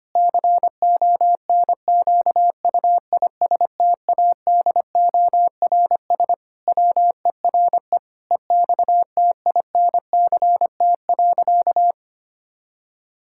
Morse code, heard as a keyed tone, about 25 wpm, 700 Hz. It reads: CONQUISTADORS WERE EXTINCT.
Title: Pangram Challenge v2 - 25wpm